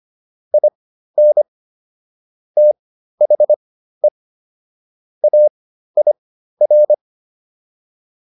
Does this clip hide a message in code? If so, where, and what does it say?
Yes: Morse code from 0.5 to 6.9 s, heard as a keyed tone, about 25 wpm, spaced as for 13 wpm, 600 Hz. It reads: IN THE AIR